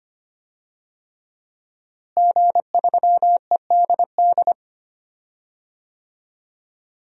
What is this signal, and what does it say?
Morse code, heard as a keyed tone, about 25 wpm, 700 Hz. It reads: G3EDD
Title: DXCC-100 Call Signs - 5 Characters - Repeated 3x - 25wpm